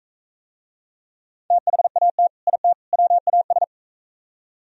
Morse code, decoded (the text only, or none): THAT IT WAS